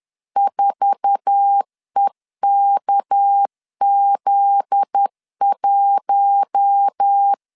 Morse code, heard as a keyed tone, about 10 wpm, 800 Hz. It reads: 4EKZ1
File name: LEZIONI / 8 YJ45 / 8 YJ4 10wpm 2633